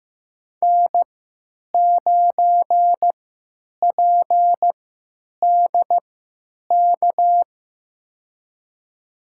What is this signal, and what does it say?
Morse code, heard as a keyed tone, about 15 wpm, spaced as for 9 wpm, 700 Hz. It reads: N9PDK
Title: USA Call Signs - 5 Characters - 3x Character Spacing - 15wpm